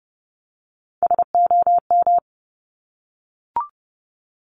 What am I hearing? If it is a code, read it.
Morse code, heard as a keyed tone, about 31 wpm, 690 Hz. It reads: SOM
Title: Top Three Letter Combinations in Words 30wpm